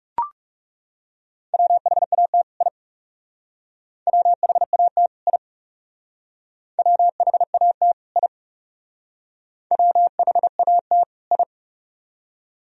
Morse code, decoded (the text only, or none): WHAT I WHAT I WHAT I WHAT I